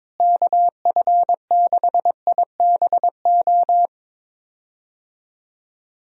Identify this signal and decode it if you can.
Morse code, heard as a keyed tone, about 22 wpm, 700 Hz. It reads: KF6IBO
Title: USA Call Signs - 6 Characters - Repeated 2x - 22wpm